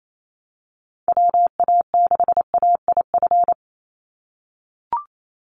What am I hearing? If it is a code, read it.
Morse code, heard as a keyed tone, about 28 wpm, 700 Hz. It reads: WA6AIF
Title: USA Call Signs 28wpm